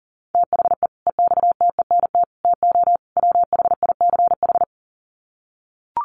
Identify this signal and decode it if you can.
Morse code, heard as a keyed tone, about 40 wpm, 700 Hz. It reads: THE EXTENT TO WHICH
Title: Sets of 4 Words 40wpm